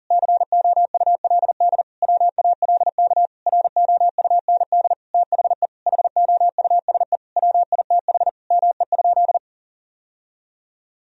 Morse code, heard as a keyed tone, about 38 wpm, 700 Hz. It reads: COULD WALK ROUND THE HOUSE WITH ME?
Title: Sentences from Top 1000 Words - Review Part 1 - 40wpm